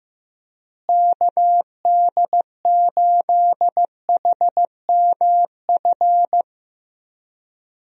Morse code, heard as a keyed tone, about 15 wpm, 700 Hz. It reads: KD8HMF
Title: USA Call Signs - 6 Characters - Repeated 2x - 15wpm